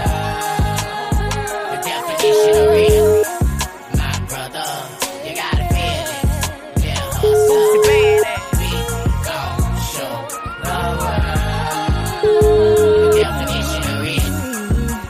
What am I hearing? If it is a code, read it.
Ringback tone (Europe, 425 Hz), with noise.